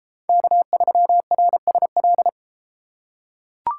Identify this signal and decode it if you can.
Morse code, heard as a keyed tone, about 31 wpm, 700 Hz. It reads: K3RSL